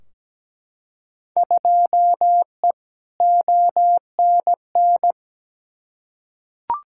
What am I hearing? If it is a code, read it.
Morse code, heard as a keyed tone, about 17 wpm, 700 Hz. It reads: 2E ONN
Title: Field Day Exchanges Encore - 17wpm